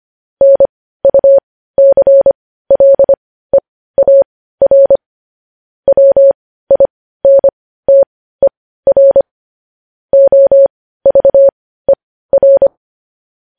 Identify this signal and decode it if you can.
Morse code, heard as a keyed tone, about 25 wpm, spaced as for 15 wpm, 550 Hz. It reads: NUCLEAR WINTER OVER
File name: podcast / News headlines-1x-25x15